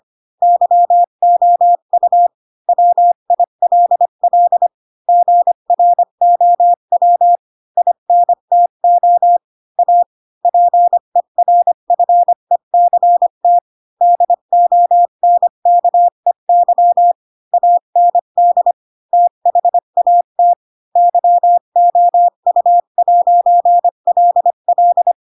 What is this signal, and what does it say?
Morse code, heard as a keyed tone, about 25 wpm, 700 Hz. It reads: YOU WILL GROW INTO A PERFECT DONKEY AND THAT YOU'LL
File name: Pinocchio / Pinocchio ch4 20 25 ik1hge